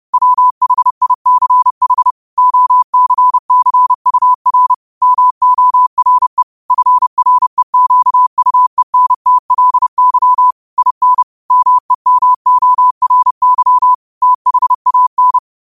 Morse code, heard as a keyed tone, about 30 wpm, 1000 Hz. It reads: WHICH OCCUR MORE FREQUENTLY IN MEMORY THAN
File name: qotdfiles / 30wpm